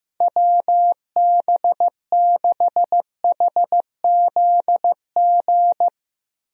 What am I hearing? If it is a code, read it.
Morse code, heard as a keyed tone, about 15 wpm, 700 Hz. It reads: WB6HZG
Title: USA Call Signs - 6 Characters - Repeated 3x - 15wpm